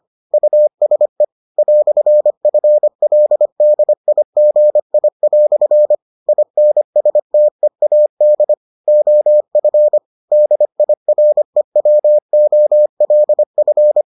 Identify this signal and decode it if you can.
Morse code, heard as a keyed tone, about 25 wpm, 600 Hz. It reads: USE "FLDIGI" INSTEAD OF DIREWOLF